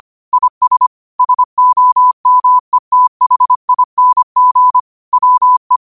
Morse code, heard as a keyed tone, about 25 wpm, 1000 Hz. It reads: IS SOMETHING WE
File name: qotdfiles / 25wpm